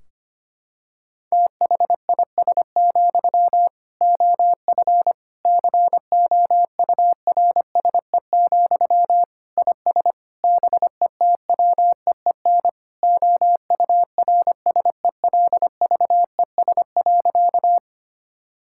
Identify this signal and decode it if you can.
Morse code, heard as a keyed tone, about 25 wpm, 700 Hz. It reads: THIS, OF COURSE, IS BETWEEN OURSELVES.
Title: Sentences from Top 1600 Words 25wpm